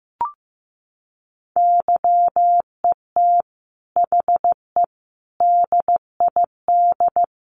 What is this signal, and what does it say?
Morse code, heard as a keyed tone, about 15 wpm, 700 Hz. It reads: YET HE DID
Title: Sentences from Top 800 Words 15wpm